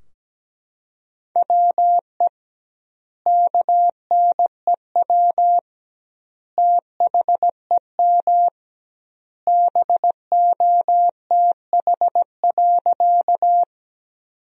Morse code, heard as a keyed tone, about 17 wpm, 700 Hz. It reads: WE KNEW THEM BOTH.
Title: 2x Word-Spacing - Sentences from Top 400 Words - 17wpm